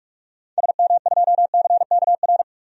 Morse code, heard as a keyed tone, about 46 wpm, 700 Hz. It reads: SM2CKR
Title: DXCC-100 Call Signs - 6 Characters - 45wpm